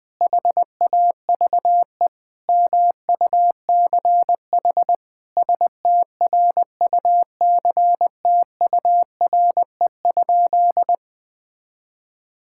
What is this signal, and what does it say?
Morse code, heard as a keyed tone, about 20 wpm, 700 Hz. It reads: HAVE MUCH STRUCTURE?
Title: Sentences from Top 1000 Words - Review Part 6 - 20wpm